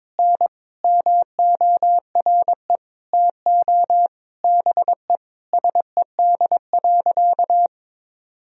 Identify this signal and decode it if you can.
Morse code, heard as a keyed tone, about 21 wpm, 700 Hz. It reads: N MORE TO BE SED.
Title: Sentences from Top 100 Words and Abbreviations 22wpm